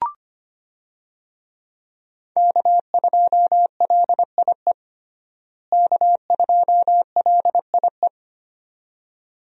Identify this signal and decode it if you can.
Morse code, heard as a keyed tone, about 25 wpm, 700 Hz. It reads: K2LIE K2LIE
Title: USA Call Signs - 5 Characters - Repeated 2x - 25wpm